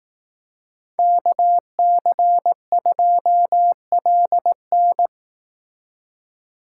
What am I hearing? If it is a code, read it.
Morse code, heard as a keyed tone, about 18 wpm, 700 Hz. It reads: KC2LN
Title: Speed Racing - USA 2x2 Call Signs 15wpm